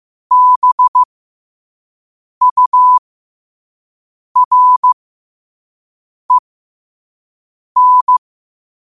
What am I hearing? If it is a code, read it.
Morse code, heard as a keyed tone, about 15 wpm, spaced as for 5 wpm, 1000 Hz. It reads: BUREN
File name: qotdfiles / 5wpm